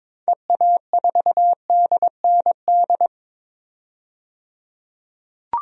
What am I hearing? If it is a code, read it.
Morse code, heard as a keyed tone, about 22 wpm, 700 Hz. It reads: EA4DND
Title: DXCC-100 Call Signs - 6 Characters - 22wpm